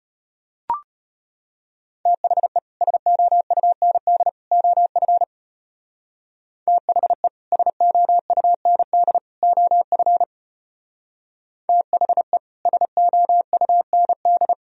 Morse code, heard as a keyed tone, about 33 wpm, 700 Hz. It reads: THE SOUND OF THE SOUND OF THE SOUND